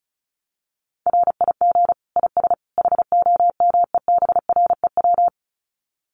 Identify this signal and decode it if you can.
Morse code, heard as a keyed tone, about 34 wpm, 700 Hz. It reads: RIG IS HOMEBREW
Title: Common QSO Phrases 35wpm